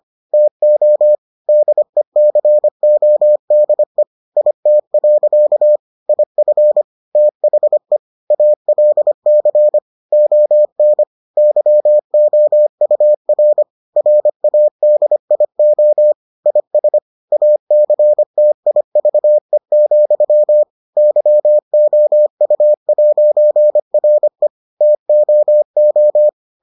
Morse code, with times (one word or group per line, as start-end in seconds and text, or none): 0.3-1.1 TO
1.5-4.0 DECODE
4.4-5.8 IT.
6.1-6.8 IF
7.2-8.0 THE
8.3-9.8 ALC
10.1-11.0 ON
11.4-13.6 YOUR
14.0-16.1 RADIO
16.5-17.0 IS
17.3-20.6 ACTIVE,
21.0-24.5 YOU'RE
24.8-26.3 TOO